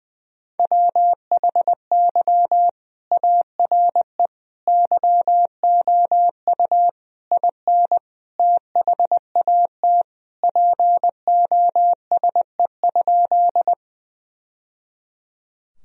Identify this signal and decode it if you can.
Morse code, heard as a keyed tone, about 21 wpm, 700 Hz. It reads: WHY ARE YOU IN THAT POSE?
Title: Sentences from Top 1000 Words - Review Part 2 - 20wpm